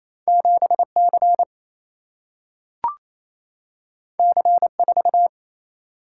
Morse code, heard as a keyed tone, about 28 wpm, 700 Hz. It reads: 7C C4
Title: Single-Letter Single-Number 28wpm